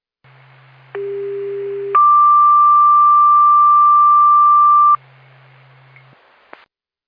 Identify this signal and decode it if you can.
Two-tone page, heard as a one-second tone, then a three-second tone, about 400 then 1180 Hz. It is heard from 1.0 to 5.0 s.